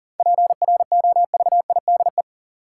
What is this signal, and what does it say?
Morse code, heard as a keyed tone, about 40 wpm, 700 Hz. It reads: PROVIDE
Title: Sentences from Top 1100 Words - New Word Drill Alt - 40wpm